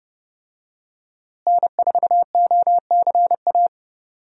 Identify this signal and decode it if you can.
Morse code, heard as a keyed tone, about 30 wpm, 700 Hz. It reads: N4OCA
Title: USA Call Signs - 5 Characters - Repeated 3x - 30wpm